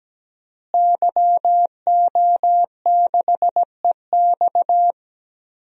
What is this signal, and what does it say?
Morse code, heard as a keyed tone, about 17 wpm, 700 Hz. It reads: YO6EX
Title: DXCC-100 Call Signs - 5 Characters - Repeated 3x - 17wpm